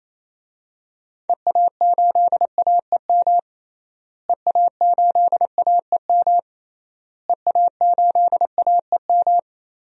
Morse code, heard as a keyed tone, about 28 wpm, 700 Hz. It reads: EA8AEM EA8AEM EA8AEM
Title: DXCC-100 Call Signs - 6 Characters - Repeated 3x - 28wpm